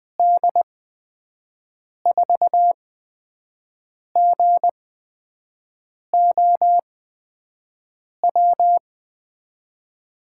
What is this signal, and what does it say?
Morse code, heard as a keyed tone, about 20 wpm, spaced as for 5 wpm, 700 Hz. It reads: D4GOW
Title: USA Call Signs - 6 Characters - 8x Character Spacing - 20wpm